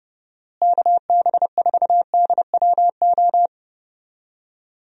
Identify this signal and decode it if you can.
Morse code, heard as a keyed tone, about 32 wpm, 700 Hz. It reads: KB4DWO
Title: USA Call Signs - 6 Characters - Repeated 2x - 30wpm